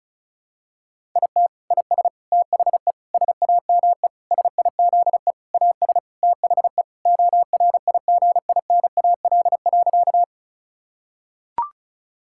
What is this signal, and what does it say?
Morse code, heard as a keyed tone, about 36 wpm, 690 Hz. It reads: IT IS THE SAME SIZE AS THE ORIGINAL.
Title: Sentences from Top 1000 Words - Review Part 6 - 35wpm